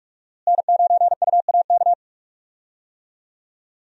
Morse code, heard as a keyed tone, about 44 wpm, 700 Hz. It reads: N9UAK
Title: USA Call Signs - 5 Characters - Repeated 2x - 45wpm